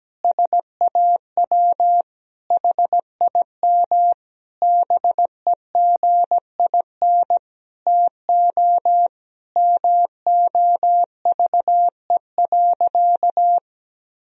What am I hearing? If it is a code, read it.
Morse code, heard as a keyed tone, about 17 wpm, 700 Hz. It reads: SAW HIM BEGIN TO MOVE.
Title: Sentences from Top 1000 Words - Review Part 2 - 17wpm